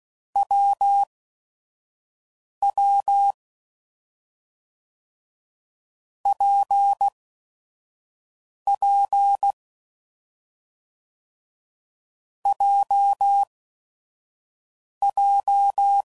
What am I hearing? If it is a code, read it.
Morse code, heard as a keyed tone, about 16 wpm, spaced as for 5 wpm, 800 Hz. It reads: WW PP JJ
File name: Combinations / Difficult Combos / WPJ WPJ1